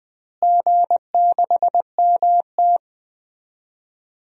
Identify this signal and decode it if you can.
Morse code, heard as a keyed tone, about 21 wpm, 700 Hz. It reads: G6MT
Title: DXCC-100 Call Signs - 4 Characters - Repeated 3x - 20wpm